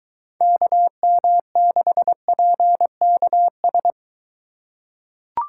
Morse code, heard as a keyed tone, about 22 wpm, 700 Hz. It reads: KM6PKS